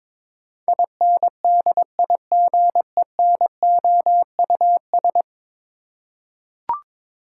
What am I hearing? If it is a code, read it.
Morse code, heard as a keyed tone, about 22 wpm, 700 Hz. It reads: INDIGENOUS
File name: practice / 22wpm / 10 Letter Words - First 3 Letters Given - 22wpm